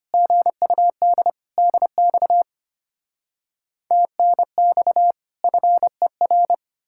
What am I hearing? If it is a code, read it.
Morse code, heard as a keyed tone, about 33 wpm, 700 Hz. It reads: GUD DX TNX FER